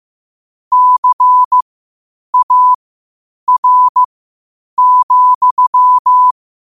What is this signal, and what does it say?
Morse code, heard as a keyed tone, about 15 wpm, spaced as for 8 wpm, 1000 Hz. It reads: CAR,
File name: qotdfiles / 8wpm